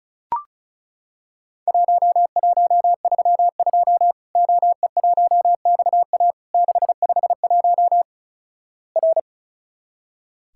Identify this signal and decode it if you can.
Morse code, heard as a keyed tone, about 36 wpm, 700 Hz. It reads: 1132 OE1XA 651 R
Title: Worked All Europe DX Contest - QTC Exchange - 35wpm